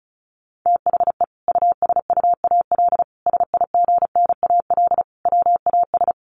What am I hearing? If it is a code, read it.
Morse code, heard as a keyed tone, about 34 wpm, 700 Hz. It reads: THE USUAL SIGNAL WAS
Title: Sentences from Top 1400 Words 35wpm